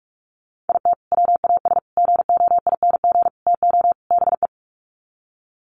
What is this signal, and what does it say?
Morse code, heard as a keyed tone, about 44 wpm, 700 Hz. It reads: IT WAS GOING TO BE
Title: Sets of 5 Words 45wpm